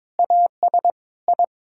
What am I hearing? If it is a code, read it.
Morse code, heard as a keyed tone, about 22 wpm, 700 Hz. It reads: AS I
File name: practice / 22wpm / Sets of 7 Words - 22wpm